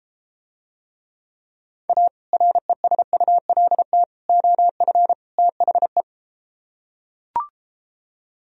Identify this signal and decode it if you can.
Morse code, heard as a keyed tone, about 33 wpm, 700 Hz. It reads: A RESULT OF THE